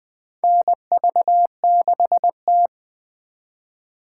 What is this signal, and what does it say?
Morse code, heard as a keyed tone, about 20 wpm, 700 Hz. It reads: NV6T
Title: USA Call Signs - 4 Characters - 4x Character Spacing - 20wpm